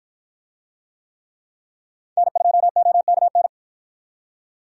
Morse code, heard as a keyed tone, about 53 wpm, 700 Hz. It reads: N2OKN